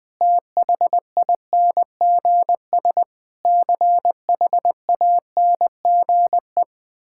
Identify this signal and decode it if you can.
Morse code, heard as a keyed tone, about 20 wpm, 700 Hz. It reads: THINGS CHANGE
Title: Sentences with 8-Word Phrases - 20wpm